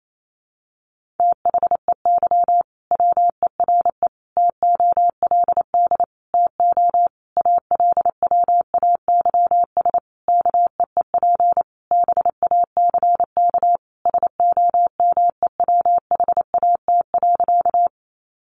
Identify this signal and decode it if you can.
Morse code, heard as a keyed tone, about 28 wpm, 700 Hz. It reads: THEY WERE TOLD TO ALWAYS KEEP BACK SOMEWHAT.
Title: Sentences from Top 1300 Words 28wpm